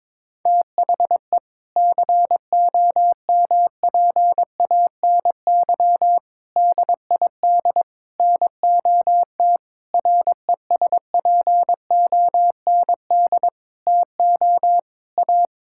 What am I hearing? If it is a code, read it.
Morse code, heard as a keyed tone, about 22 wpm, 700 Hz. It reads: THE COMPANY DID NOT RESPOND TO A
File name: practice / 22wpm / Sentences with 8-Word Phrases - 22wpm